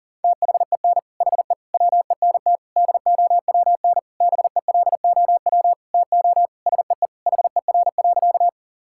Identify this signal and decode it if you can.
Morse code, heard as a keyed tone, about 40 wpm, 700 Hz. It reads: THEN HE WENT DOWN BELOW TO SEE HER.
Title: Sentences from Top 1000 Words - Review Part 5 - 40wpm